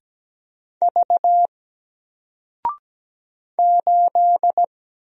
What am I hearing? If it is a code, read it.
Morse code, heard as a keyed tone, about 17 wpm, 700 Hz. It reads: V 8
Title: Single Character (Letter:Number:Punctuation:Prosign) 17wpm